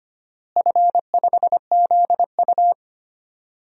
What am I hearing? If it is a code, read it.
Morse code, heard as a keyed tone, about 25 wpm, 700 Hz. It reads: F5ZU
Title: DXCC-100 Call Signs - 4 Characters - Repeated 3x - 25wpm